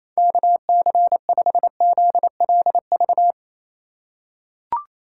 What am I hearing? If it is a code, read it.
Morse code, heard as a keyed tone, about 27 wpm, 700 Hz. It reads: KC5ZLV
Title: USA Call Signs - 3x Character Spacing - 28wpm